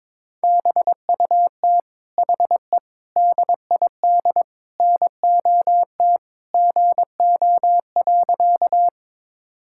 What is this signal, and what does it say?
Morse code, heard as a keyed tone, about 22 wpm, 700 Hz. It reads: BUT HE DID NOT GO.
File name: practice / 22wpm / Sentences with 4-Word Phrases - 22wpm